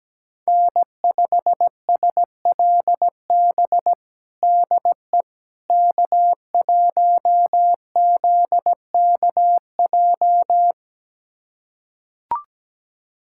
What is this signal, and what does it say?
Morse code, heard as a keyed tone, about 17 wpm, 700 Hz. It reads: N5SLB DE K1ZKJ
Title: USA Callsign DE Callsign 17wpm